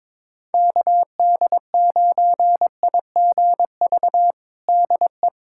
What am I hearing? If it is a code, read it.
Morse code, heard as a keyed tone, about 22 wpm, 700 Hz. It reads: KD9IGV DE
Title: USA Callsign DE Callsign - 6 Characters - 22wpm